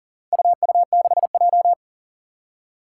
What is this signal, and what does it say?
Morse code, heard as a keyed tone, about 40 wpm, 700 Hz. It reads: UU6J
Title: Four Challenging Letter-Number Combinations 40wpm